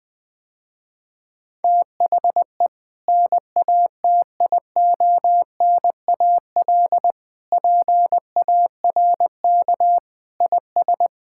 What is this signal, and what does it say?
Morse code, heard as a keyed tone, about 21 wpm, 700 Hz. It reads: THE NATIONAL PARK IS
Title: Sentences with 2-Word Phrases - Spoken Beginning - 20wpm